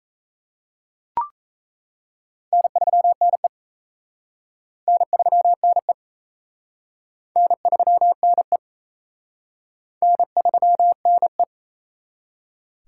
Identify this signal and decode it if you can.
Morse code, heard as a keyed tone, about 37 wpm, 700 Hz. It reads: N3NE N3NE N3NE N3NE